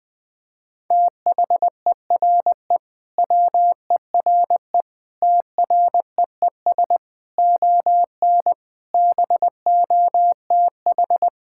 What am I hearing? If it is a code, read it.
Morse code, heard as a keyed tone, about 21 wpm, 700 Hz. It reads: THERE WERE TREES ON BOTH
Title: Sentences with 5-Word Phrases - 20wpm